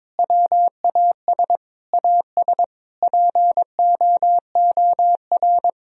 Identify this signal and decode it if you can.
Morse code, heard as a keyed tone, about 22 wpm, 700 Hz. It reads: WAS AS POOR
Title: Sentences from Top 1000 Words - Review Part 2 - 22wpm